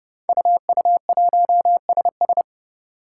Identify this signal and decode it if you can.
Morse code, heard as a keyed tone, about 30 wpm, 700 Hz. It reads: UU1SS
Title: Five Challenging Letter-Number Combinations 30wpm